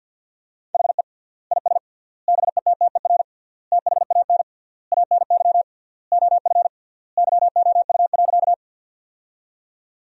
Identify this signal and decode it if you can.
Morse code, heard as a keyed tone, about 50 wpm, 700 Hz. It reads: HE IS BETTER THAN ANY OF YOU.
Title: 3x Word-Spacing - Sentences from Top 400 Words - 50wpm